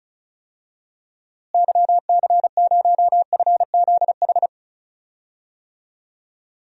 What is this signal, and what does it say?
Morse code, heard as a keyed tone, about 36 wpm, 700 Hz. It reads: YC0FZH
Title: DXCC-100 Call Signs - 6 Characters - Repeated 2x - 35wpm